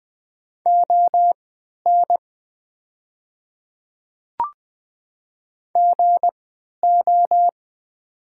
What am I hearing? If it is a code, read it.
Morse code, heard as a keyed tone, about 20 wpm, spaced as for 11 wpm, 700 Hz. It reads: ON GO
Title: 3x Character Spacing - Top 300 Words - 20wpm